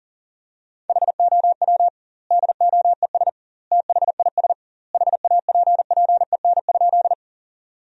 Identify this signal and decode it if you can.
Morse code, heard as a keyed tone, about 40 wpm, 700 Hz. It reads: HOW DOES THIS HAPPEN?
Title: 2x Word-Spacing - Sentences from Top 400 Words - 40wpm